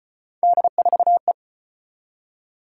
Morse code, heard as a keyed tone, about 34 wpm, 700 Hz. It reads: D4E